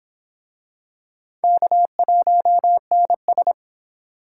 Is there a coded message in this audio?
Yes: Morse code from 1.4 to 3.5 s, heard as a keyed tone, about 25 wpm, 700 Hz. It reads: K1NS